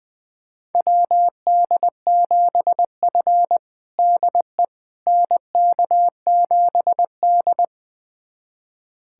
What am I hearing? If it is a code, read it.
Morse code, heard as a keyed tone, about 20 wpm, 700 Hz. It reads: WD7F DE NK7D
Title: USA Callsign DE Callsign - 4 Characters - 20wpm